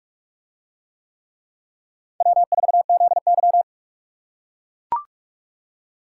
Morse code, heard as a keyed tone, about 44 wpm, 700 Hz. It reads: W4ZY